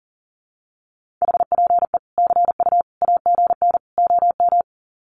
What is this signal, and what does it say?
Morse code, heard as a keyed tone, about 40 wpm, 700 Hz. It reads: HPE CU AGN OM
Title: Common QSO Phrases 40wpm